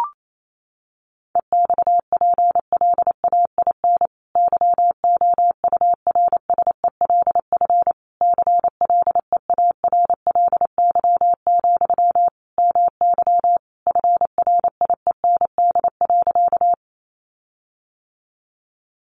Morse code, heard as a keyed tone, about 28 wpm, 700 Hz. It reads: EXPLAIN YOURSELF CLEARLY, MY FRIEND.
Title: Sentences from Top 1300 Words 28wpm